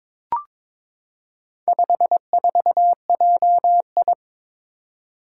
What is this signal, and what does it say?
Morse code, heard as a keyed tone, about 23 wpm, 690 Hz. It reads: 54JI
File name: practice / 22wpm / Four Letter-Number Combinations - 22wpm